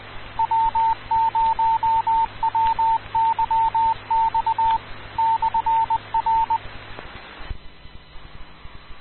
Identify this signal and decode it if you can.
Morse code, heard as a keyed tone, about 20 wpm, 900 Hz, with radio noise. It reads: W0WYX /R
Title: Rocky Mountain Radio League Repeaters